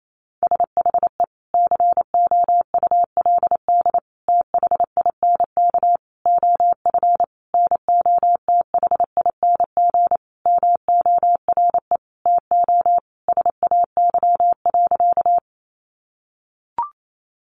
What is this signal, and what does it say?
Morse code, heard as a keyed tone, about 28 wpm, 700 Hz. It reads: SHE COULD THINK OF NOTHING MORE TO SAY.
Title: Sentences from Top 500 Words 28wpm